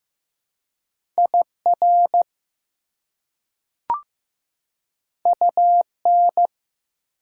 Morse code, heard as a keyed tone, about 15 wpm, 700 Hz. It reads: IR UN
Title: Prefixes and Suffixes 15wpm